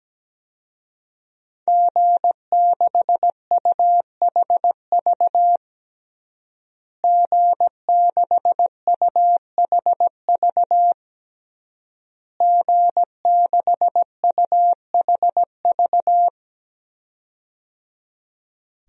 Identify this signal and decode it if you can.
Morse code, heard as a keyed tone, about 17 wpm, 700 Hz. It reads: G6UHV G6UHV G6UHV
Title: DXCC-100 Call Signs - 5 Characters - Repeated 3x - 17wpm